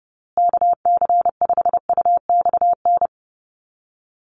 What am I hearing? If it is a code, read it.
Morse code, heard as a keyed tone, about 28 wpm, 700 Hz. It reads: KC5UXN
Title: USA Call Signs Repeated 2x 30wpm